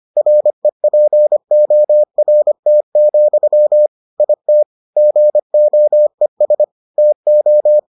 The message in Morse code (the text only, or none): REPORT, IT GOES TO